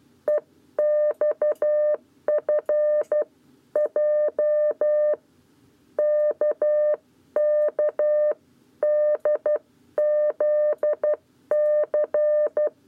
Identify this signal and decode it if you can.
Morse code, heard as a keyed tone, about 12 wpm, 600 Hz. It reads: EXF J KK DZC
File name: 7b lezione cw 1